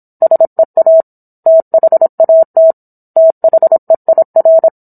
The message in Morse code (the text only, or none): SEA THAT THEIR